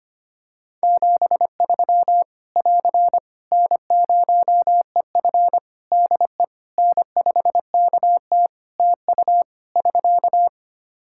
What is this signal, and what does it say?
Morse code, heard as a keyed tone, about 25 wpm, 700 Hz. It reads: 73 <AR> N0EF DE N5KT TU <SK>
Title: Practice QSO - Exchange 4 - Ending - 25wpm